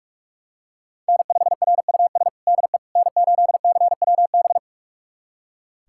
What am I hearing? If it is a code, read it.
Morse code, heard as a keyed tone, about 46 wpm, 700 Hz. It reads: N5RUS DE N8CWB
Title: USA Callsign DE Callsign - 5 Characters - 45wpm